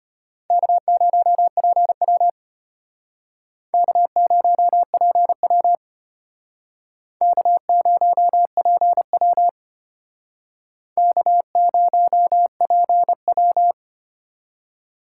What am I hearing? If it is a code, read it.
Morse code, heard as a keyed tone, about 35 wpm, 700 Hz. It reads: K0PW K0PW K0PW K0PW